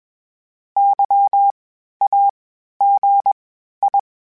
Morse code, heard as a keyed tone, about 21 wpm, spaced as for 12 wpm, 800 Hz. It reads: YAGI
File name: (spaced) / FromSMs spaced mp3s / Antennas sp@25